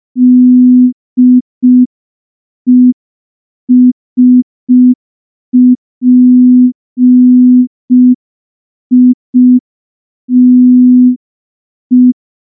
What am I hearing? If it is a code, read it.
Morse code, heard as a keyed tone, about 5 wpm, 250 Hz. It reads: DESPITE
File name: morse / morse-training-05WPM